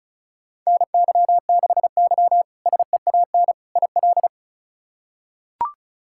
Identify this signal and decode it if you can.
Morse code, heard as a keyed tone, about 36 wpm, 700 Hz. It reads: NY6Y SEAN IL
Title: K1USN SST QSO Practice - 35wpm